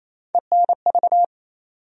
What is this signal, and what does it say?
Morse code, heard as a keyed tone, about 28 wpm, 700 Hz. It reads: ENV
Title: USA Call Sign Postfixes - 3 Letters 28wpm